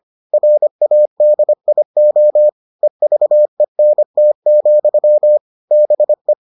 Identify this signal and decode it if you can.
Morse code, heard as a keyed tone, about 25 wpm, 600 Hz. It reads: RADIO EVENT, BE